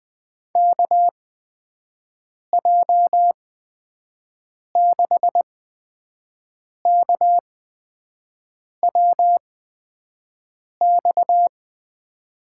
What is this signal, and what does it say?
Morse code, heard as a keyed tone, about 20 wpm, spaced as for 5 wpm, 700 Hz. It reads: KJ6KWX
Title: USA Call Signs - 6 Characters - 8x Character Spacing - 20wpm